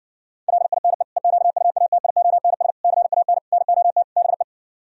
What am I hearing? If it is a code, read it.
Morse code, heard as a keyed tone, about 60 wpm, 700 Hz. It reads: LINE EQUATIONS CAN NOT BE